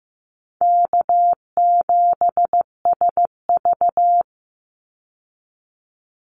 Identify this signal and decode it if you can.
Morse code, heard as a keyed tone, about 15 wpm, 700 Hz. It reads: K7SV
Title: Easy Call Signs Repeated 3x - 15wpm